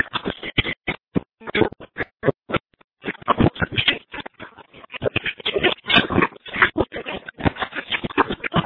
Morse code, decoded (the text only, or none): none